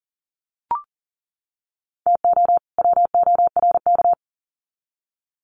Morse code, heard as a keyed tone, about 40 wpm, 700 Hz. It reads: TO WORK
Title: Sets of 2 Words 40wpm